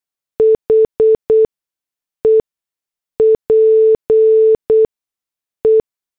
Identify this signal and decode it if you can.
Morse code, heard as a keyed tone, about 8 wpm, spaced as for 6 wpm, 440 Hz. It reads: HE PE
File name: morse / headlines